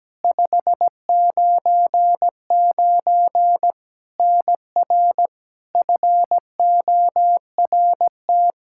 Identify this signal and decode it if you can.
Morse code, heard as a keyed tone, about 17 wpm, 700 Hz. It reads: 599 NR FORT
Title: Practice QSO - Exchange 1 - Beginning - 17wpm